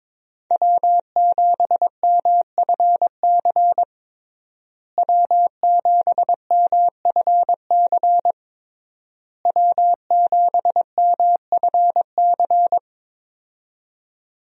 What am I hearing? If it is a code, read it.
Morse code, heard as a keyed tone, about 22 wpm, 700 Hz. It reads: W7MFC W7MFC W7MFC
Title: USA Call Signs - 5 Characters - Repeated 3x - 22wpm